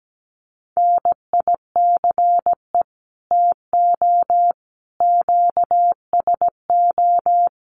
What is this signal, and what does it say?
Morse code, heard as a keyed tone, about 17 wpm, 700 Hz. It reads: NICE TO QSO
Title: Common QSO Phrases 17wpm